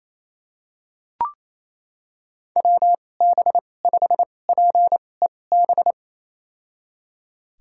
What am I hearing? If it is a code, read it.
Morse code, heard as a keyed tone, about 28 wpm, spaced as for 20 wpm, 700 Hz. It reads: WB5PEB
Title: USA Call Signs - 6 Characters - 2x Character Spacing - 28wpm